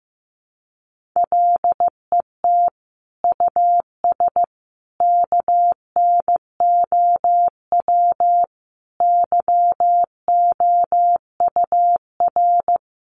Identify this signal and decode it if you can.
Morse code, heard as a keyed tone, about 15 wpm, 700 Hz. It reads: LET US KNOW YOUR
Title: Sentences from Top 1400 Words 15wpm